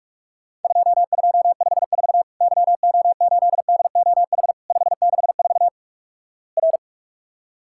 Morse code, heard as a keyed tone, about 44 wpm, 700 Hz. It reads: 2254 YO8DOH 564 R
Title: Worked All Europe DX Contest - QTC Exchange - 45wpm